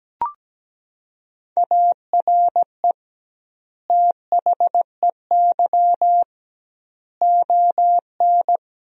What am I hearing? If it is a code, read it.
Morse code, heard as a keyed tone, about 17 wpm, 700 Hz. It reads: ARE THEY ON
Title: 2x Word-Spacing - Sentences from Top 500 Words - 17wpm